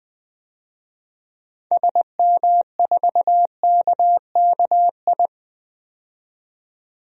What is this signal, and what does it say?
Morse code, heard as a keyed tone, about 21 wpm, 700 Hz. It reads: SM4KKI
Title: DXCC-100 Call Signs - 6 Characters - 20wpm